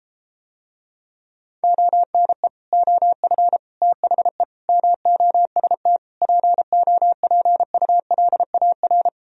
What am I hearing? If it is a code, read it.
Morse code, heard as a keyed tone, about 33 wpm, 700 Hz. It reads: ONE OF THE MOST POPULAR